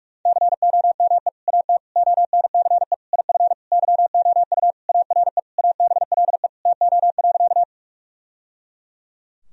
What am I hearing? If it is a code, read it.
Morse code, heard as a keyed tone, about 47 wpm, 700 Hz. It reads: COME AT ONCE IF YOU ARE ABLE TO.
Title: Sentences from Top 1000 Words - Review Part 2 - 45wpm